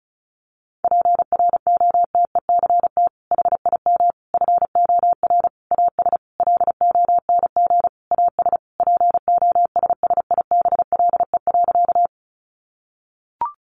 Morse code, heard as a keyed tone, about 34 wpm, 700 Hz. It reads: PROTECT HIM FOR AS LONG AS POSSIBLE.
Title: Sentences from Top 900 Words 35wpm